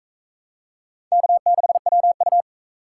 Morse code, heard as a keyed tone, about 42 wpm, 700 Hz. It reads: K6WU